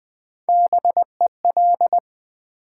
Morse code, heard as a keyed tone, about 20 wpm, 700 Hz. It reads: BEL
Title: Build A Word - 1 Letter At A Time - 3 to 5-Letter Words - 20wpm